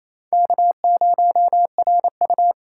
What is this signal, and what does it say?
Morse code, heard as a keyed tone, about 28 wpm, 700 Hz. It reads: K0RU
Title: K1USN SST QSO Practice - 28wpm